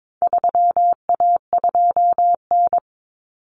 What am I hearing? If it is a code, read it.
Morse code, heard as a keyed tone, about 22 wpm, 700 Hz. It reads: 3A2N
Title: Common Cut Numbers - 4-Digit 22wpm